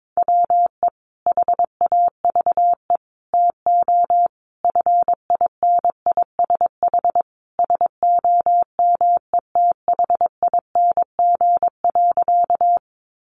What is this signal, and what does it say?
Morse code, heard as a keyed tone, about 22 wpm, 700 Hz. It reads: WE HAVE TO FINISH SOMETHING.
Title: Sentences from Top 1200 Words 22wpm